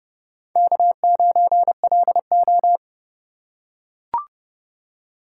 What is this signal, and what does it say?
Morse code, heard as a keyed tone, about 29 wpm, 700 Hz. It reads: K9LO